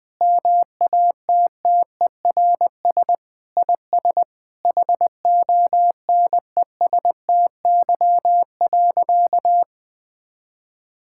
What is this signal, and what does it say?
Morse code, heard as a keyed tone, about 20 wpm, 700 Hz. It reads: MATTERS IS HONESTY.
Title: Sentences with 5-Word Phrases - 20wpm